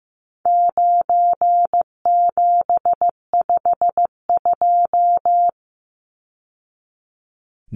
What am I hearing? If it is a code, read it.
Morse code, heard as a keyed tone, about 15 wpm, 700 Hz. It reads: 9752
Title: Numbers - 4-Digit 15wpm